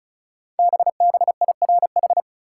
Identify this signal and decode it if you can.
Morse code, heard as a keyed tone, about 36 wpm, 700 Hz. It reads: BBIRH
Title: Code Groups - 5 Letters - 35wpm